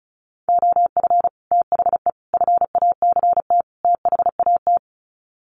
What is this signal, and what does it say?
Morse code, heard as a keyed tone, about 34 wpm, 700 Hz. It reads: OF THE FACT THAT
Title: Sets of 4 Words 35wpm